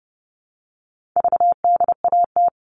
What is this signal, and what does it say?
Morse code, heard as a keyed tone, about 30 wpm, 700 Hz. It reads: VDAT
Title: All Cut Numbers - 4-Digit 30wpm